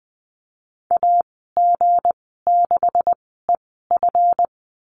Morse code, heard as a keyed tone, about 20 wpm, spaced as for 14 wpm, 700 Hz. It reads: AG6EF
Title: 161 Lesson 26g - Letter g - Call Signs 20wpm